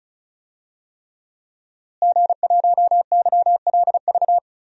Morse code, heard as a keyed tone, about 34 wpm, 700 Hz. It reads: G1YLV
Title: DXCC-100 Call Signs - 5 Characters - Repeated 3x - 35wpm